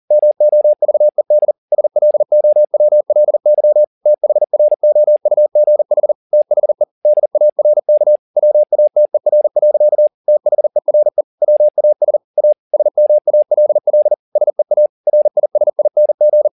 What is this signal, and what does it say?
Morse code, heard as a keyed tone, about 40 wpm, 600 Hz. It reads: MOVED SLOWLY THROUGH THE DARK WATER. THERE WAS A SMALL SEA RISING